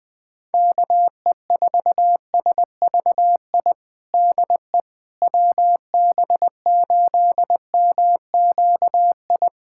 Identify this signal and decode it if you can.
Morse code, heard as a keyed tone, about 20 wpm, 700 Hz. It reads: KE4SVI DE WB8MQI
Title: USA Callsign DE Callsign - 6 Characters - 20wpm